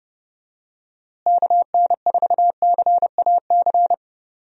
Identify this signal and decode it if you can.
Morse code, heard as a keyed tone, about 30 wpm, 700 Hz. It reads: KN4CAC
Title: USA Call Signs - 6 Characters - Repeated 2x - 30wpm